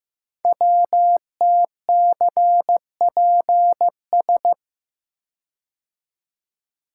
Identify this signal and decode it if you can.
Morse code, heard as a keyed tone, about 15 wpm, 700 Hz. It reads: WTCPS
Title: Code Groups - 5 Letters - 15wpm